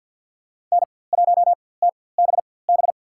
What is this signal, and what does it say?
Morse code, heard as a keyed tone, about 48 wpm, spaced as for 23 wpm, 700 Hz. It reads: N1TBB
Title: USA Call Signs - 4x Character Spacing - 50wpm